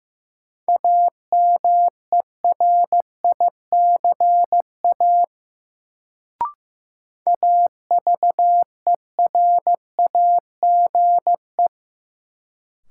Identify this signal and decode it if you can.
Morse code, heard as a keyed tone, about 15 wpm, 700 Hz. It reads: AMERICA AVERAGE
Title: Sets of 5 Words - New Word Drill - 15wpm